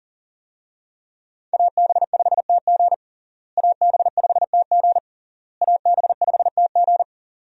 Morse code, heard as a keyed tone, about 40 wpm, 700 Hz. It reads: AB5TG AB5TG AB5TG
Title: USA Call Signs - 5 Characters - Repeated 3x - 40wpm